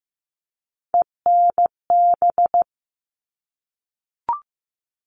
Morse code, heard as a keyed tone, about 15 wpm, 700 Hz. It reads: ENB
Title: All Cut Numbers - 3-Digit 15wpm